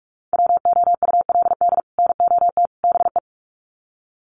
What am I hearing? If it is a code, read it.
Morse code, heard as a keyed tone, about 44 wpm, 700 Hz. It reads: WOULD NOT BE
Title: Sets of 3 Words 45wpm